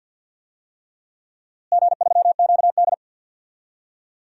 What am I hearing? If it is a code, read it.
Morse code, heard as a keyed tone, about 50 wpm, 700 Hz. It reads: G3QD
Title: DXCC-100 Call Signs - 4 Characters - Repeated 3x - 50wpm